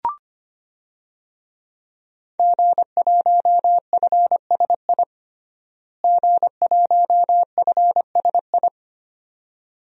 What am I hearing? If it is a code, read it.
Morse code, heard as a keyed tone, about 25 wpm, 700 Hz. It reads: G1FSI G1FSI
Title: DXCC-100 Call Signs - 5 Characters - Repeated 2x - 25wpm